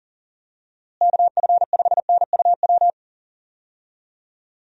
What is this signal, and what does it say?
Morse code, heard as a keyed tone, about 40 wpm, 700 Hz. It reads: KF5NUW
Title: USA Call Signs - 6 Characters - Repeated 2x - 40wpm